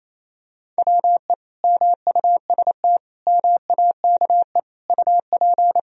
Morse code, heard as a keyed tone, about 28 wpm, 700 Hz. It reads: WE MUST MAKE UP
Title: Sentences from Top 1000 Words - Review Part 3 - 28wpm